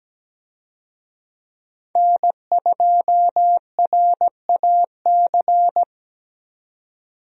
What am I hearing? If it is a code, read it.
Morse code, heard as a keyed tone, about 17 wpm, 700 Hz. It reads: N2RAC
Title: USA Call Signs - 5 Characters - Repeated 3x - 17wpm